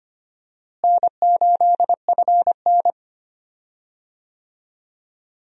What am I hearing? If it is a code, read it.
Morse code, heard as a keyed tone, about 25 wpm, 700 Hz. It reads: N8FN
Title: USA Call Signs - 4 Characters - Repeated 2x - 25wpm